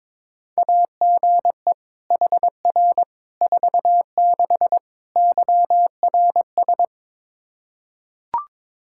Morse code, heard as a keyed tone, about 22 wpm, 700 Hz. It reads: AGE HR 46 YRS
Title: Practice QSO - Exchange 3 - More Info - 22wpm